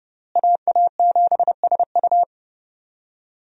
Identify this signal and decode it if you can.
Morse code, heard as a keyed tone, about 30 wpm, 700 Hz. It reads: AA7SU
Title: USA Call Signs - 5 Characters - Repeated 3x - 30wpm